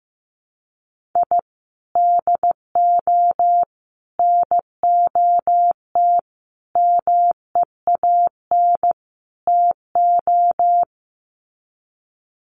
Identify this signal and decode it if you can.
Morse code, heard as a keyed tone, about 15 wpm, 700 Hz. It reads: I DO NOT MEAN TO
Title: Sets of 5 Words 15wpm